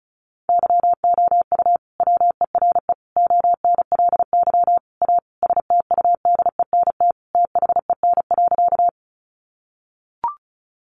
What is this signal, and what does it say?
Morse code, heard as a keyed tone, about 34 wpm, 700 Hz. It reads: YOU WERE ONLY A STUDENT THEN.
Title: Sentences from Top 1000 Words 35wpm